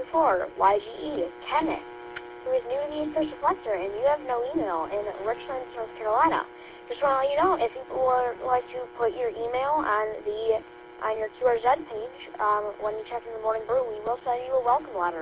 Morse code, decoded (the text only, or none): none